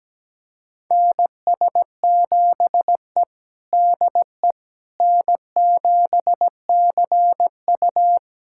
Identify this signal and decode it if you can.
Morse code, heard as a keyed tone, about 17 wpm, 700 Hz. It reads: NS7E DE N7CU
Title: USA Callsign DE Callsign - 4 Characters - 17wpm